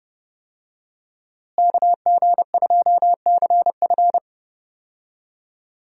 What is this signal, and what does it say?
Morse code, heard as a keyed tone, about 28 wpm, 700 Hz. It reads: KG2CF
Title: USA Call Signs - 5 Characters - Repeated 3x - 30wpm